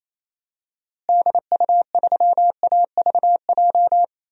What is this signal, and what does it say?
Morse code, heard as a keyed tone, about 27 wpm, 700 Hz. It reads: DU3AVJ
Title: DXCC-100 Call Signs - 6 Characters - Repeated 2x - 28wpm